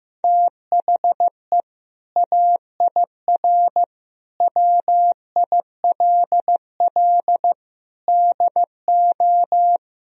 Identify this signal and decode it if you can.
Morse code, heard as a keyed tone, about 15 wpm, 700 Hz. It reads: THE AIR WILL DO
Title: Sentences from Top 1000 Words - Review Part 1 - 15wpm